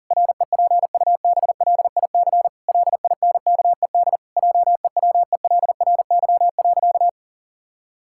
Morse code, heard as a keyed tone, about 40 wpm, 700 Hz. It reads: REPUBLIC LINKED JEWELRY.
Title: Pangram Challenge v2 - 40wpm